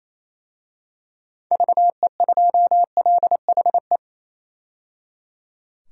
Morse code, heard as a keyed tone, about 28 wpm, 700 Hz. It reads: VE2LHE
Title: DXCC-100 Call Signs - 6 Characters - 28wpm